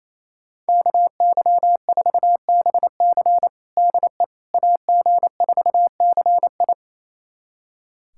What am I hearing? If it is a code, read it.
Morse code, heard as a keyed tone, about 28 wpm, 700 Hz. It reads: KY4BC DE AG4CI
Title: USA Callsign DE Callsign - 5 Characters - 28wpm